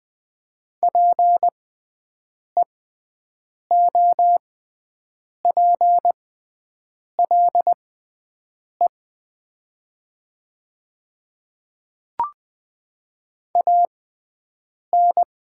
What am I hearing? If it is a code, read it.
Morse code, heard as a keyed tone, about 20 wpm, spaced as for 7 wpm, 700 Hz. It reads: PEOPLE AN
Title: 6x Character Spacing - Top 100 Words - 20wpm